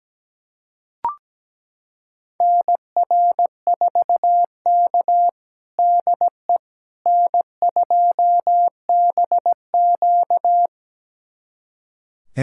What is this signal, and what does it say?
Morse code, heard as a keyed tone, about 17 wpm, 700 Hz. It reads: NR4K DE N2BQ
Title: USA Callsign DE Callsign - 4 Characters - 17wpm